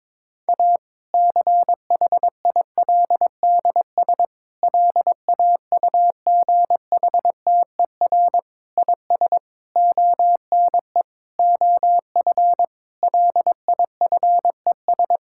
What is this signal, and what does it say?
Morse code, heard as a keyed tone, about 22 wpm, 700 Hz. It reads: A CHILDS LAUGHTER IS ONE OF LIFES
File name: Difficult Sentences with Hints - 22wpm